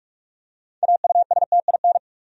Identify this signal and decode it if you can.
Morse code, heard as a keyed tone, about 48 wpm, 700 Hz. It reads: AUSTIN
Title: USA Common Names - 45wpm